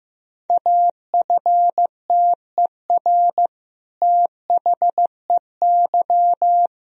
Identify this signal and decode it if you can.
Morse code, heard as a keyed tone, about 15 wpm, 700 Hz. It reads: AFTER THEY
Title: Sentences from Top 1000 Words - Review Part 2 - 15wpm